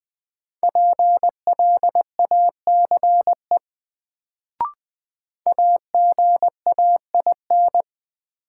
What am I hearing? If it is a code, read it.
Morse code, heard as a keyed tone, about 21 wpm, 700 Hz. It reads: PLACE AGAIN
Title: Sets of 4 Words - New Word Drill - 20wpm